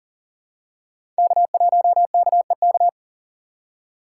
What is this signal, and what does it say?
Morse code, heard as a keyed tone, about 40 wpm, 690 Hz. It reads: K1KEK
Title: USA Call Signs - 3x Character Spacing - 40wpm